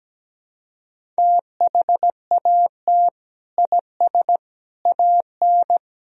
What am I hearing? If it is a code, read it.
Morse code, heard as a keyed tone, about 17 wpm, 700 Hz. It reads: THAT IS AN
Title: Sentences from Top 1000 Words - Review Part 1 - 17wpm